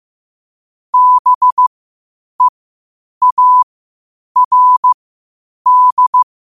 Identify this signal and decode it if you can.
Morse code, heard as a keyed tone, about 15 wpm, spaced as for 8 wpm, 1000 Hz. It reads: BEARD
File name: qotdfiles / 8wpm